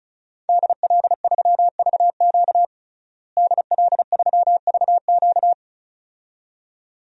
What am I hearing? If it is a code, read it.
Morse code, heard as a keyed tone, about 35 wpm, 700 Hz. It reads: DL3VQ DL3VQ
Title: DXCC-100 Call Signs - 5 Characters - Repeated 2x - 35wpm